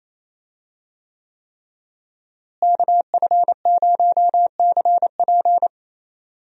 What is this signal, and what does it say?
Morse code, heard as a keyed tone, about 28 wpm, 700 Hz. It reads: KF0CP
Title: USA Call Signs - 5 Characters - Repeated 3x - 28wpm